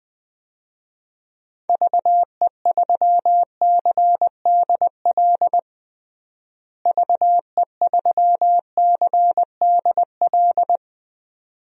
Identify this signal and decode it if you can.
Morse code, heard as a keyed tone, about 20 wpm, 700 Hz. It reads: VE3CDL VE3CDL
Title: DXCC-100 Call Signs - 6 Characters - Repeated 2x - 20wpm